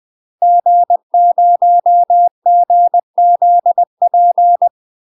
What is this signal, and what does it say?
Morse code, heard as a keyed tone, about 20 wpm, 700 Hz. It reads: G0GZP
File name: UK MASTER QSOs TRAINING/20WPM